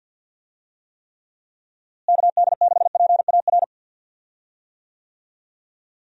Morse code, heard as a keyed tone, about 46 wpm, 700 Hz. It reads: KD6PAR
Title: USA Call Signs - 6 Characters - Repeated 3x - 50wpm